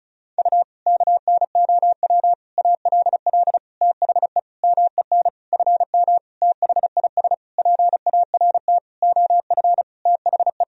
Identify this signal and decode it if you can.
Morse code, heard as a keyed tone, about 34 wpm, 700 Hz. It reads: U KNOW ALL THE MEN FM THIS PART OF THE
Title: Sentences from Top 300 Words and Abbreviations 35wpm